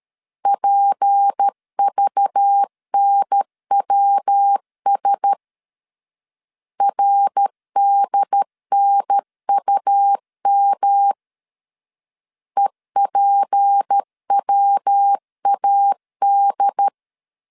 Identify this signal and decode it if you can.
Morse code, heard as a keyed tone, about 13 wpm, 800 Hz. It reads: PVNWS RDNUM EPWAD
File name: LEZIONI / 6 PKX0 / 6 P 12wpm 2033